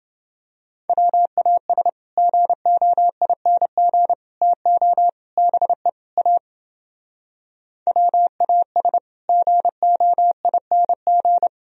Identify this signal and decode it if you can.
Morse code, heard as a keyed tone, about 30 wpm, 700 Hz. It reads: WAS GOING TO BE A WAS GOING